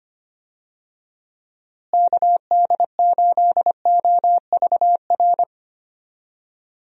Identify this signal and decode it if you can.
Morse code, heard as a keyed tone, about 25 wpm, 700 Hz. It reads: KD8OVR
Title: USA Call Signs - 6 Characters - Repeated 2x - 25wpm